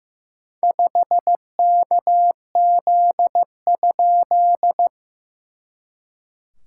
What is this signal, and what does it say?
Morse code, heard as a keyed tone, about 15 wpm, 700 Hz. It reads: 5KZ?
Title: SOTA QSO Practice 15wpm